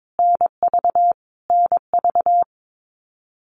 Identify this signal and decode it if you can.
Morse code, heard as a keyed tone, about 22 wpm, 700 Hz. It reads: NV NV
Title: US States Repeated 3x 22wpm